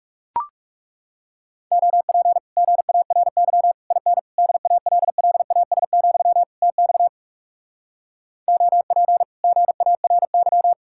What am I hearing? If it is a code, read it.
Morse code, heard as a keyed tone, about 40 wpm, 700 Hz. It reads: OP GARY IN DALLAS, TX OP GARY